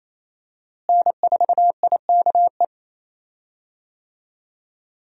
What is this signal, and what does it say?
Morse code, heard as a keyed tone, about 27 wpm, 700 Hz. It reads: N4IKE
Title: USA Call Signs - 5 Characters - Repeated 3x - 28wpm